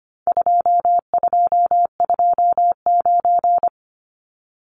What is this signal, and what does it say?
Morse code, heard as a keyed tone, about 25 wpm, 700 Hz. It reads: 2229
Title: Numbers - 4-Digit 25wpm